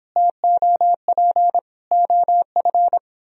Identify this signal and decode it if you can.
Morse code, heard as a keyed tone, about 23 wpm, 700 Hz. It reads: TOP OF